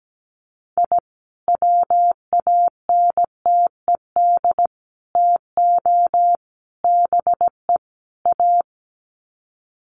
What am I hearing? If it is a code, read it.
Morse code, heard as a keyed tone, about 17 wpm, 700 Hz. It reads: I WANTED TO BE A
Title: Sets of 5 Words Encore2 17wpm